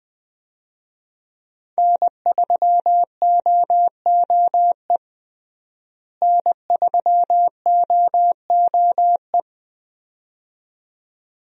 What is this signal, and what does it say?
Morse code, heard as a keyed tone, about 20 wpm, 700 Hz. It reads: N3OOE N3OOE
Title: USA Call Signs - 5 Characters - Repeated 2x - 20wpm